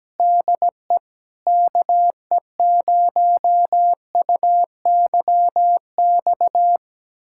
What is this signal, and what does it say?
Morse code, heard as a keyed tone, about 17 wpm, 700 Hz. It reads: DE KE0UYX
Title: USA Callsign DE Callsign - 6 Characters - 17wpm